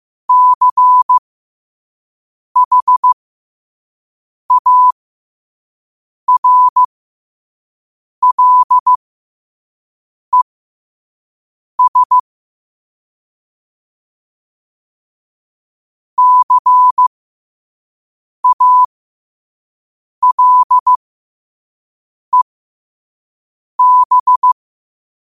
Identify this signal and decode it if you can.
Morse code, heard as a keyed tone, about 15 wpm, spaced as for 5 wpm, 1000 Hz. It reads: CHARLES CALEB
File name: qotdfiles / 5wpm